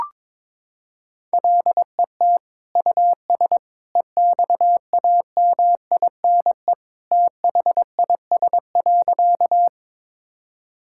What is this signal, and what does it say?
Morse code, heard as a keyed tone, about 22 wpm, 700 Hz. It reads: LET US EXAMINE THIS.
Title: Sentences from Top 1600 Words 22wpm